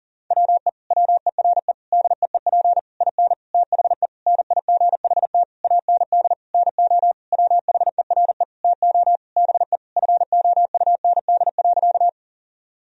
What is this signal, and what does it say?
Morse code, heard as a keyed tone, about 37 wpm, 700 Hz. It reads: WE WERE DEEP IN THE NIGHT AND NO WHERE TO BE FOUND.
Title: Sentences from Top 1000 Words - Review Part 3 - 40wpm